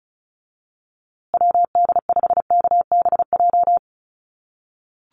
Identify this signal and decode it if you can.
Morse code, heard as a keyed tone, about 34 wpm, 700 Hz. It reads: WD5KBJ
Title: USA Call Signs Repeated 2x 35wpm